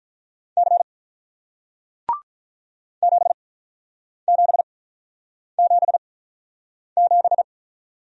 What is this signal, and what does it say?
Morse code, heard as a keyed tone, about 46 wpm, spaced as for 9 wpm, 700 Hz. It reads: C 7777